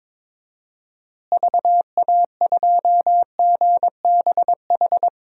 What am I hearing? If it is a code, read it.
Morse code, heard as a keyed tone, about 22 wpm, 700 Hz. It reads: VA2GBH
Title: DXCC-100 Call Signs - 6 Characters - Repeated 3x - 22wpm